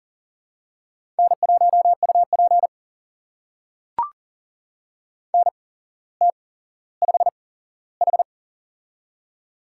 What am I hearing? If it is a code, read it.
Morse code, heard as a keyed tone, about 39 wpm, spaced as for 17 wpm, 700 Hz. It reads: N1UP NT5H
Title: USA Call Signs - 8x Character Spacing - 40wpm